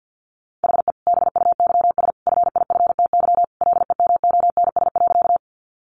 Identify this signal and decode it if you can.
Morse code, heard as a keyed tone, about 50 wpm, 700 Hz. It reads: HE BUYS FIFTY LEMONS.
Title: Additional Easy Phrases 50wpm